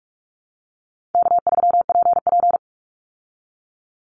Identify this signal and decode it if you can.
Morse code, heard as a keyed tone, about 44 wpm, 700 Hz. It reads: K3PP
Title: Easy Call Signs Repeated 2x - 45wpm